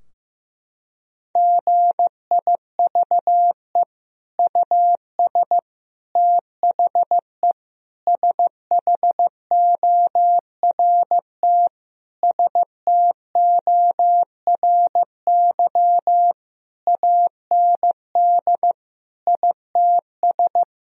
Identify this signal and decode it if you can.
Morse code, heard as a keyed tone, about 15 wpm, 700 Hz. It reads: GIVE US THE SHORT STORY AND ITS
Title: Sentences from Top 1000 Words - Review Part 4 - 15wpm